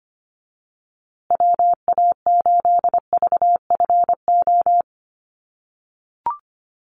Morse code, heard as a keyed tone, about 25 wpm, 700 Hz. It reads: WA8VFO
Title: USA Call Signs Repeated 3x 25wpm